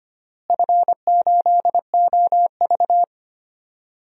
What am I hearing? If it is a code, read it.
Morse code, heard as a keyed tone, about 25 wpm, 700 Hz. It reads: F8OV
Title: DXCC-100 Call Signs - 4 Characters - Repeated 3x - 25wpm